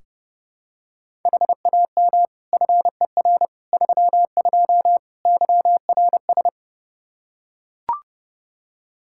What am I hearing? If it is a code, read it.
Morse code, heard as a keyed tone, about 32 wpm, 700 Hz. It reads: HAM FER 32 YRS
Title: K1USN SST QSO Practice 30wpm